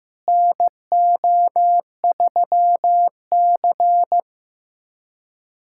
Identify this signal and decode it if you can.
Morse code, heard as a keyed tone, about 15 wpm, 700 Hz. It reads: NO3C
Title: USA Callsign DE Callsign - 4 Characters - 15wpm